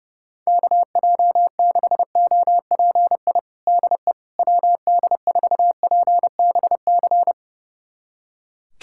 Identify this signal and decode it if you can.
Morse code, heard as a keyed tone, about 30 wpm, 700 Hz. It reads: KJ6OPI DE WD4PBC
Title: USA Callsign DE Callsign - 6 Characters - 30wpm